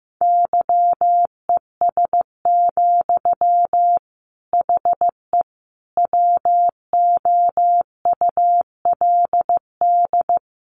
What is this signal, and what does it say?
Morse code, heard as a keyed tone, about 15 wpm, 700 Hz. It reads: YES, HE WOULD
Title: Sentences from Top 900 Words 15wpm